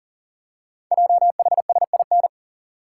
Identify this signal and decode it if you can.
Morse code, heard as a keyed tone, about 42 wpm, 700 Hz. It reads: JHSIN
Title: Code Groups - 5 Letters - 40wpm